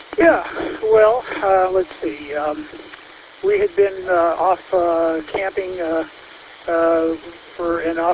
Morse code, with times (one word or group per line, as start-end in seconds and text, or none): none